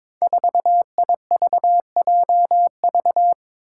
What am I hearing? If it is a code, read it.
Morse code, heard as a keyed tone, about 22 wpm, 700 Hz. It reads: 4IVJV
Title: Five Challenging Letter-Number Combinations 22wpm